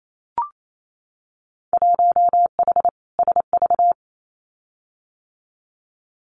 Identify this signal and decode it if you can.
Morse code, heard as a keyed tone, about 28 wpm, 700 Hz. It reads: 1H SV
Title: Winter Field Day Exchanges 28wpm